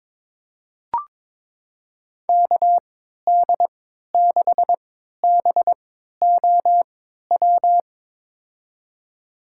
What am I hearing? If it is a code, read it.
Morse code, heard as a keyed tone, about 22 wpm, spaced as for 13 wpm, 700 Hz. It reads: KD6BOW
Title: USA Call Signs - 6 Characters - 3x Character Spacing - 22wpm